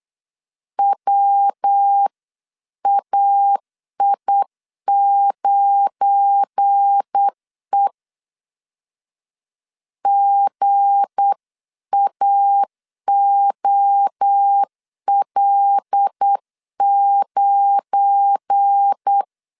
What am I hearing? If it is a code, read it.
Morse code, heard as a keyed tone, about 8 wpm, 800 Hz. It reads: WAI9E GAOL9
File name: LEZIONI / 7 GF9Z / 7 GF9 8wpm 3545